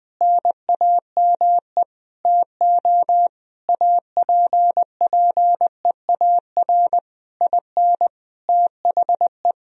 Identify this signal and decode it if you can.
Morse code, heard as a keyed tone, about 20 wpm, 700 Hz. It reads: NAME TO APPEAR IN THE
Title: Sentences from Top 1000 Words - Review Part 2 - 20wpm